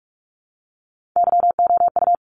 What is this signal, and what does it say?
Morse code, heard as a keyed tone, about 44 wpm, 700 Hz. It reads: YOU
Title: Top 500 Words 45wpm